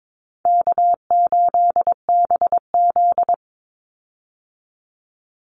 Morse code, heard as a keyed tone, about 22 wpm, 700 Hz. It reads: K8BZ
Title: Easy Call Signs Repeated 2x - 22wpm